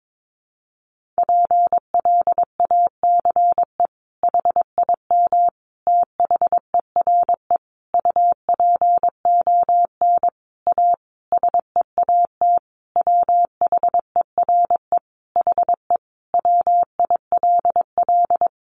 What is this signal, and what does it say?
Morse code, heard as a keyed tone, about 22 wpm, 700 Hz. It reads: PLACE HIM THERE UPON A SEAT WHERE HE WILL
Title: Sentences from Top 1500 Words 22wpm